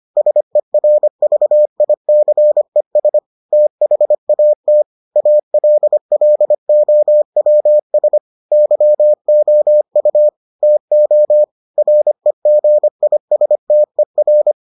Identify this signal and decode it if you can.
Morse code, heard as a keyed tone, about 25 wpm, 600 Hz. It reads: SERVICES THAT ALLOWS YOU TO REGISTER